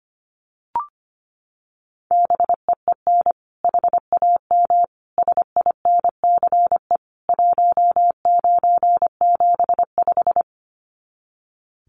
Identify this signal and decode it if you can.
Morse code, heard as a keyed tone, about 25 wpm, 700 Hz. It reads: BEEN HAM SINCE 1975
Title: Common QSO Phrases 25wpm